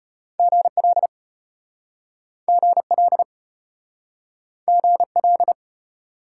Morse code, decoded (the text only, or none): GL GL GL